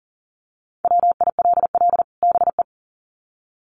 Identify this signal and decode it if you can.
Morse code, heard as a keyed tone, about 40 wpm, 700 Hz. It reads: WILL BE
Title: Sets of 2 Words 40wpm